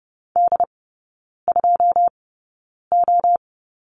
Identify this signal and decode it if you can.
Morse code, heard as a keyed tone, about 30 wpm, spaced as for 9 wpm, 700 Hz. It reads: D2O
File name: 081 Lesson 14e - Number 2 - Three Characters 30wpm